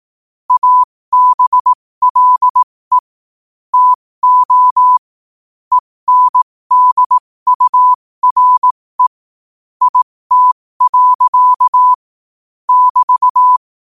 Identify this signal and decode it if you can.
Morse code, heard as a keyed tone, about 18 wpm, spaced as for 15 wpm, 1000 Hz. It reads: ABLE TO ENDURE IT. <BT>
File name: qotdfiles / 15wpm